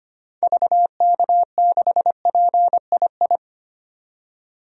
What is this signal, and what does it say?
Morse code, heard as a keyed tone, about 25 wpm, 700 Hz. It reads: VK6PII
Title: DXCC-100 Call Signs - 6 Characters - 25wpm